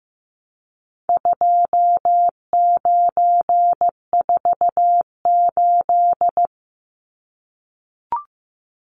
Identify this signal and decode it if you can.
Morse code, heard as a keyed tone, about 15 wpm, 700 Hz. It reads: 2948
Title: Numbers - 4-Digit 15wpm